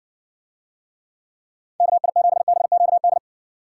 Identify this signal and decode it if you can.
Morse code, heard as a keyed tone, about 60 wpm, 700 Hz. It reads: KI7BQD